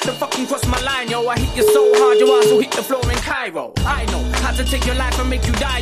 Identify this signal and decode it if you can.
Ringback tone (Europe, 425 Hz), with noise.